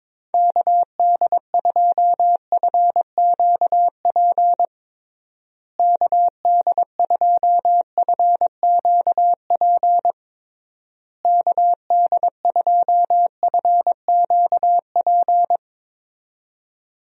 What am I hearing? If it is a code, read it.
Morse code, heard as a keyed tone, about 22 wpm, 700 Hz. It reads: KD2FQP KD2FQP KD2FQP
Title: USA Call Signs - 6 Characters - Repeated 3x - 22wpm